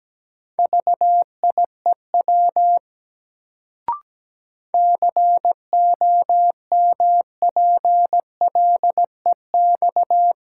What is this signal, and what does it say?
Morse code, heard as a keyed tone, about 17 wpm, 700 Hz. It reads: VIEW COMPLEX
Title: Sentences from Top 900 Words - New Word Drill - 17wpm